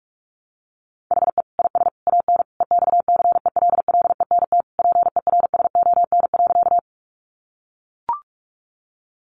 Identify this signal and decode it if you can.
Morse code, heard as a keyed tone, about 44 wpm, 700 Hz. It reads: HE IS AN EXCELLENT PERSON.
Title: Sentences from Top 1000 Words 45wpm